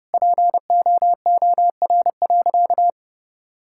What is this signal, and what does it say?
Morse code, heard as a keyed tone, about 30 wpm, 700 Hz. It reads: POOR.
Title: Sentences from Top 1000 Words - Review Part 6 - 30wpm